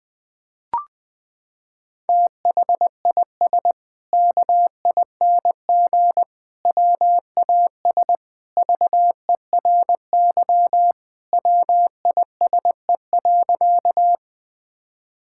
Morse code, heard as a keyed tone, about 20 wpm, 700 Hz. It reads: THIS KING WAS VERY WISE.
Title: Sentences from Top 1600 Words 20wpm